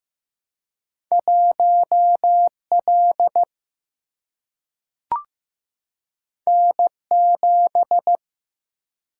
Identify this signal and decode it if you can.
Morse code, heard as a keyed tone, about 15 wpm, 700 Hz. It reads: 1 L N 7
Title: Single-Letter Single-Number 15wpm